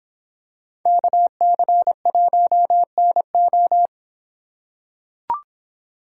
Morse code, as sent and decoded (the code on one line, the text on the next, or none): -.- -.-. .---- -. ---
KC1NO